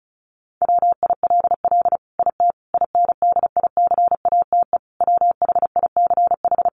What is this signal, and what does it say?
Morse code, heard as a keyed tone, about 34 wpm, 700 Hz. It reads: WILL IT INDICATE WHICH
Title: Sentences from Top 1300 Words 35wpm